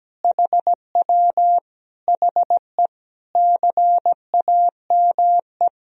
Morse code, heard as a keyed tone, about 17 wpm, 700 Hz. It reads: HW HE CAME
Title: Sentences from Top 500 Words and Abbreviations 17wpm